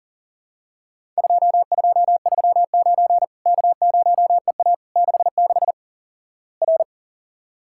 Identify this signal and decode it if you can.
Morse code, heard as a keyed tone, about 38 wpm, 700 Hz. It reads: 2239 K0EA 66 R
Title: Worked All Europe DX Contest - QTC Exchange - 40wpm